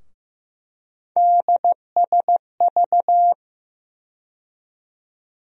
Morse code, heard as a keyed tone, about 15 wpm, 700 Hz. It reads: DSV
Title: USA Call Sign Postfixes - 3 Letters 15wpm